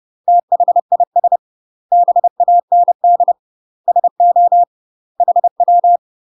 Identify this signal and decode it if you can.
Morse code, heard as a keyed tone, about 30 wpm, 700 Hz. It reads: THIS BAND SO HW
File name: UK MASTER QSOs TRAINING/30WPM